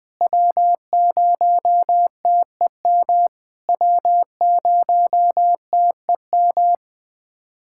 Morse code, heard as a keyed tone, about 20 wpm, 690 Hz. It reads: W0TEM W0TEM
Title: SOTA QSO Practice 20wpm